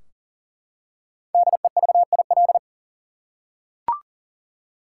Morse code, heard as a keyed tone, about 40 wpm, 700 Hz. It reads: DEVIL
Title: Five Letter Words - 40wpm